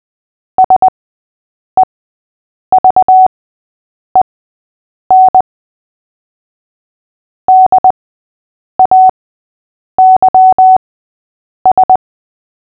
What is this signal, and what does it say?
Morse code, heard as a keyed tone, about 20 wpm, spaced as for 8 wpm, 750 Hz. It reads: SEVEN DAYS
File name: cw / fists / Short-Story-8-8wpm